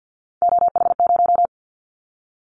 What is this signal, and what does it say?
Morse code, heard as a keyed tone, about 50 wpm, 700 Hz. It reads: OH0
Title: Callsign Prefixes for Top 100 DXCC - 50wpm